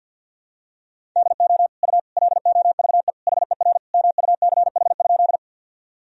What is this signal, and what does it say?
Morse code, heard as a keyed tone, about 48 wpm, 700 Hz. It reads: DO U LOVE HER MUCH?
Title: Sentences from Top 400 Words and Abbreviations 50wpm